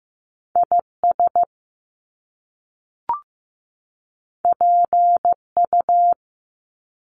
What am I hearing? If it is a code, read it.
Morse code, heard as a keyed tone, about 15 wpm, 700 Hz. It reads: IS PU
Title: Top Two Letter Combinations in Words 15wpm